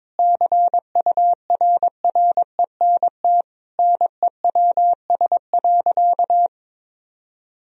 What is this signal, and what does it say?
Morse code, heard as a keyed tone, about 22 wpm, 700 Hz. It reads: CURRENT NEWS.
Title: Sentences from Top 1000 Words - Review Part 4 - 22wpm